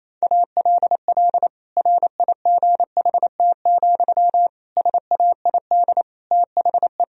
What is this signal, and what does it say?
Morse code, heard as a keyed tone, about 28 wpm, 700 Hz. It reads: ALL RIGHT, SAID THE
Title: Sentences from Top 1000 Words - Review Part 2 - 28wpm